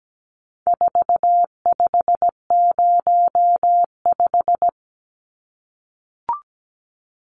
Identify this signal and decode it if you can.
Morse code, heard as a keyed tone, about 17 wpm, 700 Hz. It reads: 4505
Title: Numbers - 4-Digit 17wpm